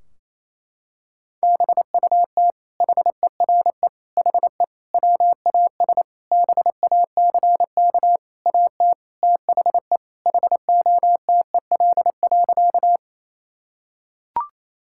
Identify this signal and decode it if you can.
Morse code, heard as a keyed tone, about 28 wpm, 700 Hz. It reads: BUT HERE HE WAS BACK AT THE HOTEL.
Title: Sentences from Top 1600 Words 28wpm